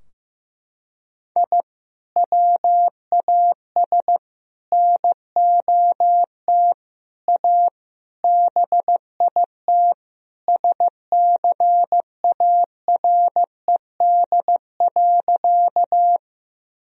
Morse code, heard as a keyed tone, about 15 wpm, 700 Hz. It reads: I WAS NOT A BIT SCARED.
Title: Sentences from Top 1000 Words - Review Part 4 - 15wpm